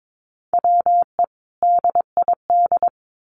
Morse code, heard as a keyed tone, about 22 wpm, 700 Hz. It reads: WE DID
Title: Sentences from Top 1100 Words 22wpm